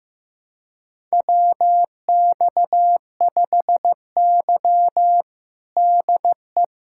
Morse code, heard as a keyed tone, about 15 wpm, 700 Hz. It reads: WX5Y DE
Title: USA Callsign DE Callsign 15wpm